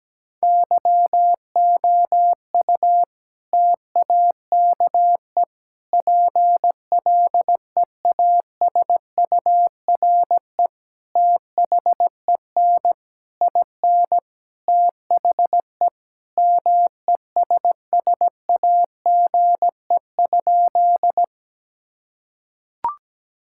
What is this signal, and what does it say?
Morse code, heard as a keyed tone, about 17 wpm, 700 Hz. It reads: YOU TAKE PLEASURE THEN IN THE MESSAGE?
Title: Sentences from Top 1600 Words 17wpm